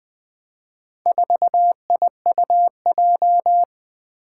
Morse code, heard as a keyed tone, about 20 wpm, 700 Hz. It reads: 4IUJ
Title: Four Challenging Letter-Number Combinations 20wpm